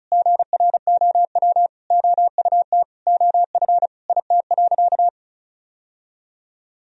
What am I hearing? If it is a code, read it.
Morse code, heard as a keyed tone, about 34 wpm, 700 Hz. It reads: GROW OUT OF IT.
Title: Sentences from Top 1000 Words - Review Part 1 - 35wpm